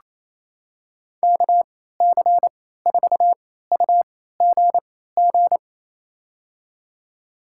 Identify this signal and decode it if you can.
Morse code, heard as a keyed tone, about 27 wpm, spaced as for 16 wpm, 700 Hz. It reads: KC4UGG
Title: USA Call Signs - 6 Characters - 3x Character Spacing - 28wpm